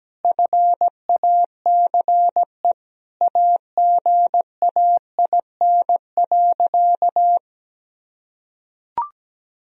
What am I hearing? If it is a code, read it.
Morse code, heard as a keyed tone, about 17 wpm, 700 Hz. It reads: FACE AGAIN.
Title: Sentences from Top 1000 Words - Review Part 2 - 17wpm